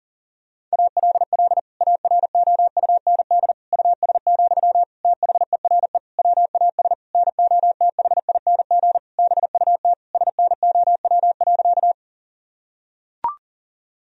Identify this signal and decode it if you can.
Morse code, heard as a keyed tone, about 39 wpm, 700 Hz. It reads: ALL AROUND US, THERE WAS NOTHING BUT SNOW.
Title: Sentences from Top 1000 Words - Review Part 3 - 40wpm